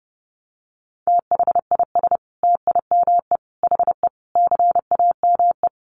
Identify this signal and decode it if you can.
Morse code, heard as a keyed tone, about 30 wpm, 700 Hz. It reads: THIS TIME HE CAME
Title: Sentences from Top 1200 Words 30wpm